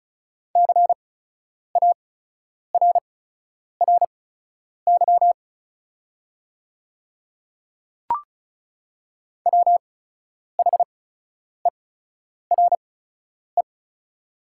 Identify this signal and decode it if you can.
Morse code, heard as a keyed tone, about 36 wpm, spaced as for 10 wpm, 690 Hz. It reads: CARRY WHERE
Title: 8x Character Spacing - Top 500 Words - 35wpm